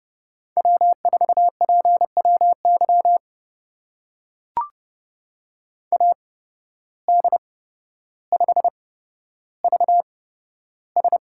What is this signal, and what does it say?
Morse code, heard as a keyed tone, about 30 wpm, spaced as for 13 wpm, 700 Hz. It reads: W4PWY AD5VS
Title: USA Call Signs - 5 Characters - 8x Character Spacing - 30wpm